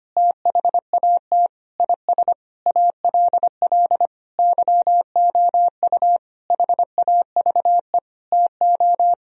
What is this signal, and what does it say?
Morse code, heard as a keyed tone, about 25 wpm, 700 Hz. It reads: THAT IS ALL YOU HAVE TO
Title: Sentences from Top 1000 Words - Review Part 2 - 25wpm